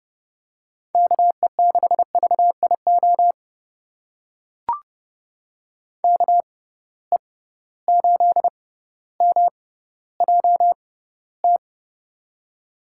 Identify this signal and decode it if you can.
Morse code, heard as a keyed tone, about 31 wpm, spaced as for 16 wpm, 700 Hz. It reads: KE6VIO KE8MJT
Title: USA Call Signs - 6 Characters - 6x Character Spacing - 30wpm